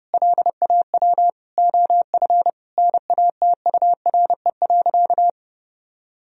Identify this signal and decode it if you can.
Morse code, heard as a keyed tone, about 30 wpm, 700 Hz. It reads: LAW OF NATURE.
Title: Sentences from Top 1000 Words - Review Part 3 - 30wpm